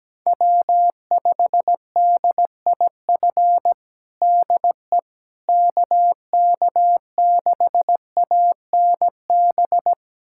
Morse code, heard as a keyed tone, about 17 wpm, 700 Hz. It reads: W5DIF DE KK6ANB
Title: USA Callsign DE Callsign 17wpm